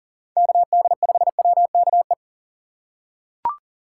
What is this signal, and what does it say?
Morse code, heard as a keyed tone, about 40 wpm, 700 Hz. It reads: KD5WKE
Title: USA Call Signs - 2x Character Spacing - 40wpm